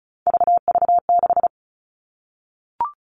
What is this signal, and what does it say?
Morse code, heard as a keyed tone, about 34 wpm, 700 Hz. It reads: VV6
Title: All Cut Numbers - 3-Digit 35wpm